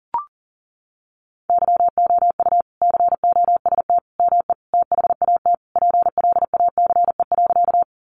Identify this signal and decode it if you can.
Morse code, heard as a keyed tone, about 40 wpm, 700 Hz. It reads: YOU COST ME THAT PLACE.
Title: Sentences from Top 800 Words 40wpm